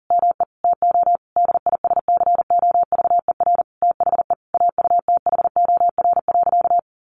Morse code, heard as a keyed tone, about 40 wpm, 700 Hz. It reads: ME TO DISCOVER THE AUTHOR.
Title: Sentences from Top 1200 Words 40wpm